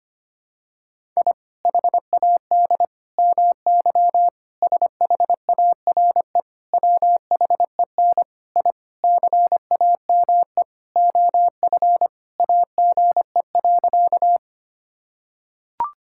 Morse code, heard as a keyed tone, about 25 wpm, 700 Hz. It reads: I HAD MY SHARE WHEN I CAME OF AGE.
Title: Sentences from Top 1000 Words - Review Part 6 - 25wpm